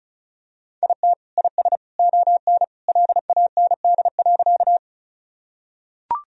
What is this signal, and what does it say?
Morse code, heard as a keyed tone, about 34 wpm, 700 Hz. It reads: IT IS ON LAND.
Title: Sentences from Top 1000 Words - Review Part 1 - 35wpm